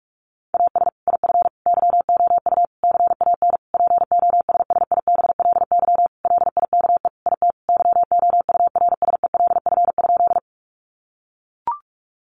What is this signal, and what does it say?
Morse code, heard as a keyed tone, about 44 wpm, 700 Hz. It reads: AS IF YOU CAN POSSIBLY LIKE IT YOURSELF?
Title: Sentences from Top 1300 Words 45wpm